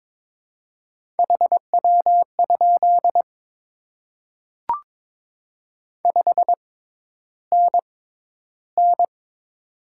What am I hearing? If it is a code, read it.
Morse code, heard as a keyed tone, about 22 wpm, spaced as for 11 wpm, 700 Hz. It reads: HW? 5NN
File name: Top 20 QSO Elements - 6x Character Spacing - 22wpm